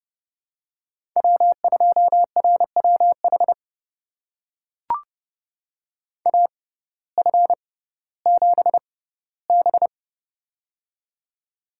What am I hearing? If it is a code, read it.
Morse code, heard as a keyed tone, about 30 wpm, spaced as for 17 wpm, 700 Hz. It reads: W2RWH AF7B
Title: USA Call Signs - 6x Character Spacing - 30wpm